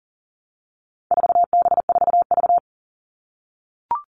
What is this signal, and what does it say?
Morse code, heard as a keyed tone, about 40 wpm, 700 Hz. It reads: 4B4V
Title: All Cut Numbers - 4-Digit 40wpm